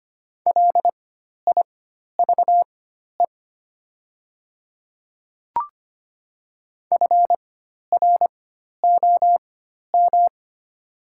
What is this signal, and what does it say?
Morse code, heard as a keyed tone, about 25 wpm, spaced as for 12 wpm, 700 Hz. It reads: LIVE FROM
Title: 4x Character Spacing - Top 200 Words - 25wpm